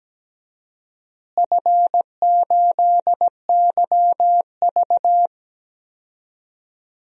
Morse code, heard as a keyed tone, about 17 wpm, 700 Hz. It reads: F8YV
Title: DXCC-100 Call Signs - 4 Characters - Repeated 3x - 17wpm